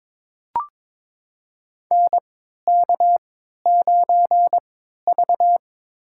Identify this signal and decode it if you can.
Morse code, heard as a keyed tone, about 22 wpm, 700 Hz. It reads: N K 9 V
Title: USA Call Signs - 4 Characters - 3x Character Spacing - 22wpm